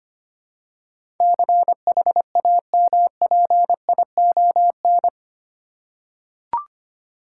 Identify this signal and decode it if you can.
Morse code, heard as a keyed tone, about 25 wpm, 700 Hz. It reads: CHAMPION
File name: Eight Letter Words - 25wpm